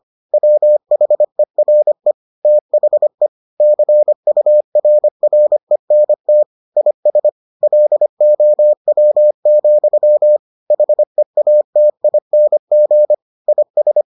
Morse code, heard as a keyed tone, about 25 wpm, 600 Hz. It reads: WHERE THE CURRENT IS LOW, HEATING IS